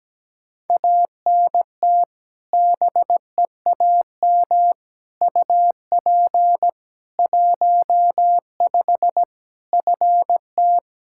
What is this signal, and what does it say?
Morse code, heard as a keyed tone, about 17 wpm, 700 Hz. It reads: ANT BEAM UP 15 FT
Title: Practice QSO - Exchange 2 - Info - 17wpm